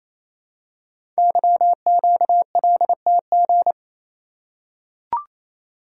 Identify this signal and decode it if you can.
Morse code, heard as a keyed tone, about 27 wpm, 700 Hz. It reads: YQLTG
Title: Code Groups - 5 Letters - 28wpm